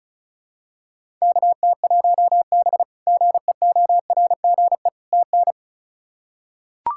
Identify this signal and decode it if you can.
Morse code, heard as a keyed tone, about 34 wpm, 700 Hz. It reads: KT1B GEORGE TN
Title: K1USN SST QSO Practice - 35wpm